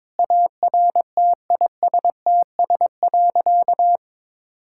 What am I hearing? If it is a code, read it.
Morse code, heard as a keyed tone, about 22 wpm, 700 Hz. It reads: ARTISTS.
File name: practice / 22wpm / Sentences with 8-Word Phrases - 22wpm